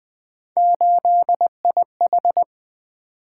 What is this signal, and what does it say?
Morse code, heard as a keyed tone, about 20 wpm, 700 Hz. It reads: 8IH
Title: Three Letter-Number Combinations - 20wpm